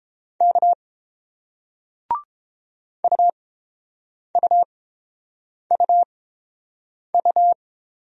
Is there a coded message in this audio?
Yes: Morse code from 0.4 to 7.5 s, heard as a keyed tone, about 30 wpm, spaced as for 7 wpm, 700 Hz. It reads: K UUUU